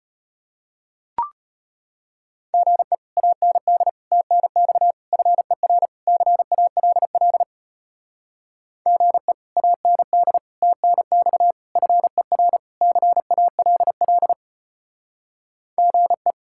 Morse code, heard as a keyed tone, about 33 wpm, 700 Hz. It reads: GE AND TNX FER CALL GE AND TNX FER CALL GE